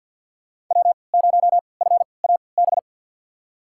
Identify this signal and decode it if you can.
Morse code, heard as a keyed tone, about 52 wpm, spaced as for 29 wpm, 700 Hz. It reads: W0FAB
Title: USA Call Signs - 5 Characters - 3x Character Spacing - 50wpm